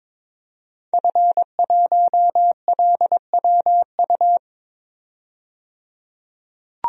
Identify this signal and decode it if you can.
Morse code, heard as a keyed tone, about 22 wpm, 700 Hz. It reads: F1LWU
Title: DXCC-100 Call Signs - 5 Characters - 22wpm